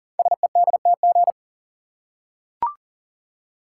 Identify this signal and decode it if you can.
Morse code, heard as a keyed tone, about 39 wpm, 700 Hz. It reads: SEDTG
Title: Code Groups - 5 Letter:Numbers - 40wpm